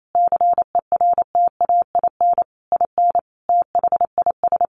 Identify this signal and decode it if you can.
Morse code, heard as a keyed tone, about 28 wpm, 700 Hz. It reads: CERTAIN IN THIS
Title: Sentences from Top 600 Words 28wpm